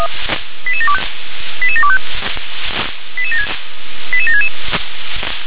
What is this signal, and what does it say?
Five-tone selective call (ZVEI), heard as quick runs of five tones, tones of about 70 ms. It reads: 88725 88725 88766